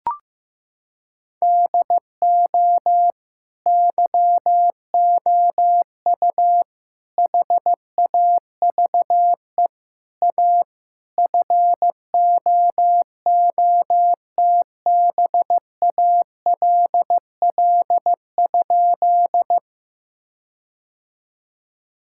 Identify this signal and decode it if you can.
Morse code, heard as a keyed tone, about 15 wpm, 700 Hz. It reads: DO YOU HAVE A FOOTBALL?
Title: Sentences from Top 1600 Words 15wpm